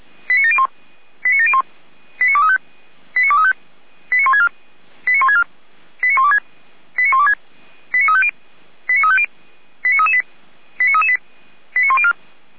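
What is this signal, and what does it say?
Five-tone selective call (ZVEI), heard as quick runs of five tones, tones of about 70 ms. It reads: 89701 89701 89235 89235 89164 89164 89126 89126 89350 89350 89338 89338 89104
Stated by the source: ZVEI five-tone selective call 89590, received by SDR.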